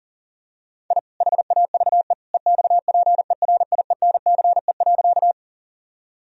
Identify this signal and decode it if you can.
Morse code, heard as a keyed tone, about 40 wpm, 700 Hz. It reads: I HAVE EXPERIENCE.
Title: Sentences from Top 1000 Words - Review Part 3 - 40wpm